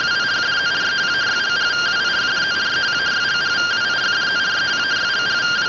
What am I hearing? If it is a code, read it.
Two-tone teleprinter signal, about 170 Hz shift.
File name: findings / 81 axial iodine 262.gen-29 rtty rx